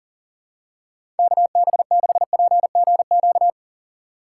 Morse code, heard as a keyed tone, about 40 wpm, 700 Hz. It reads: KB6PGQ
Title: USA Call Signs - 6 Characters - 6x Character Spacing - 40wpm